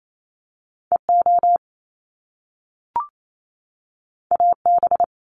Morse code, heard as a keyed tone, about 28 wpm, 700 Hz. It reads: EO AB
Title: Top Two Letter Combinations in Words 28wpm